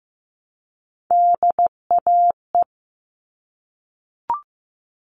Morse code, heard as a keyed tone, about 15 wpm, 700 Hz. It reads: DAE
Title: All Cut Numbers - 3-Digit 15wpm